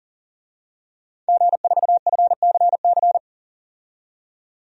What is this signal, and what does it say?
Morse code, heard as a keyed tone, about 37 wpm, 700 Hz. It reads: G4FCC
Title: DXCC-100 Call Signs - 5 Characters - 40wpm